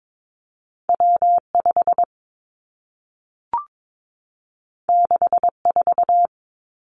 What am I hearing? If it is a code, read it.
Morse code, heard as a keyed tone, about 22 wpm, 700 Hz. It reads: W 5 6 4
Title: Two Challenging Letter-Number Combinations 22wpm